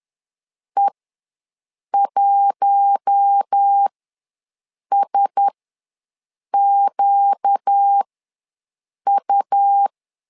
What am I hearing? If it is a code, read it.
Morse code, heard as a keyed tone, about 10 wpm, spaced as for 6 wpm, 800 Hz. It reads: E1SQU
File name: LEZIONI / 4 Q1S ws 10wpm 2208